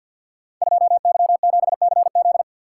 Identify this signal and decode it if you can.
Morse code, heard as a keyed tone, about 51 wpm, 700 Hz. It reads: 2Y7CZ
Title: Code Groups - 5 Letter:Numbers - 50wpm